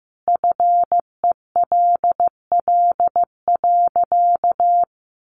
Morse code, heard as a keyed tone, about 15 wpm, 700 Hz. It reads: FELL.
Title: A Sound Of Thunder 15wpm